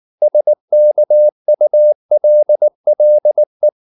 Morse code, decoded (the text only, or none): SKULLE